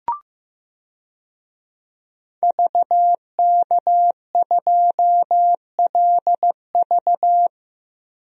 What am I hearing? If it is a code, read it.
Morse code, heard as a keyed tone, about 15 wpm, 700 Hz. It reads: VK2LV
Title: DXCC-100 Call Signs - 5 Characters - Repeated 2x - 15wpm